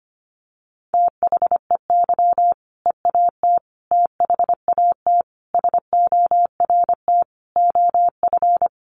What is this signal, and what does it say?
Morse code, heard as a keyed tone, about 25 wpm, 700 Hz. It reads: THEY EAT THAT SORT OF
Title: Sentences from Top 1400 Words 25wpm